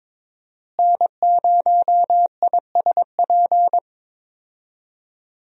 Morse code, heard as a keyed tone, about 23 wpm, 700 Hz. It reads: N0ISP
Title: USA Call Signs - 5 Characters - Repeated 3x - 22wpm